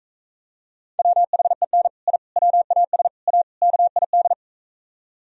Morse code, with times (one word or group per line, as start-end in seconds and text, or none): 1.0-1.9 WHEN
2.1-2.2 I
2.4-3.1 WAS
3.3-3.4 A
3.6-4.3 KID